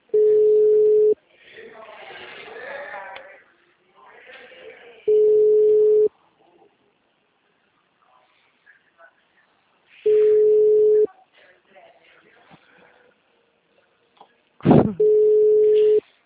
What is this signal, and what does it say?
Ringback tone (Europe, 425 Hz), with speech.
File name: out- -2005- - - .71112